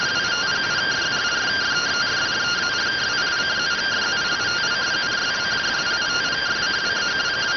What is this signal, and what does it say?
Two-tone teleprinter signal, about 170 Hz shift.